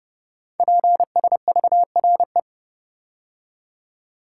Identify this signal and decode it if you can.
Morse code, heard as a keyed tone, about 30 wpm, 700 Hz. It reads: PSVRE
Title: Code Groups - 5 Letter:Numbers - 30wpm